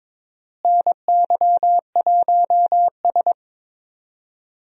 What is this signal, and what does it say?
Morse code, heard as a keyed tone, about 22 wpm, 700 Hz. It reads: NY1S
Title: USA Call Signs - 4 Characters - Repeated 2x - 22wpm